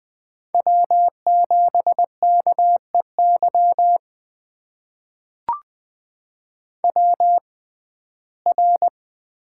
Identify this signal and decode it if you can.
Morse code, heard as a keyed tone, about 21 wpm, spaced as for 15 wpm, 700 Hz. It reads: W7KEY WR
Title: USA Call Signs - 5 Characters - 6x Character Spacing - 20wpm